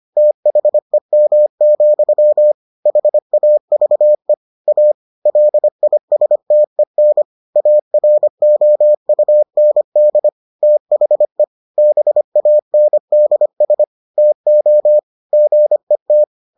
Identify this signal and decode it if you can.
Morse code, heard as a keyed tone, about 25 wpm, 600 Hz. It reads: THEM, HAVE A LISTEN AROUND THE BANDS TO GET